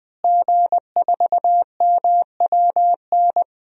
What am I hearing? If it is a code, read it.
Morse code, heard as a keyed tone, about 20 wpm, 700 Hz. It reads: G4MWN
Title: DXCC-100 Call Signs - 5 Characters - Repeated 3x - 20wpm